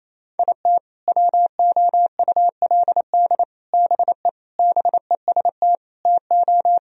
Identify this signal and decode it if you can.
Morse code, heard as a keyed tone, about 28 wpm, 700 Hz. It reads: IT WOULD BE BEST TO
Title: Sentences from Top 1000 Words - Review Part 2 - 28wpm